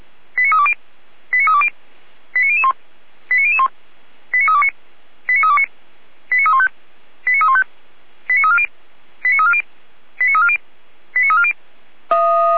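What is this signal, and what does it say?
Five-tone selective call (ZVEI), heard as quick runs of five tones, tones of about 70 ms. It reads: 89320 89320 89001 89001 89329 89329 89315 89315 89340 89340 89340 89340
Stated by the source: ZVEI five-tone selective call received on an SDR.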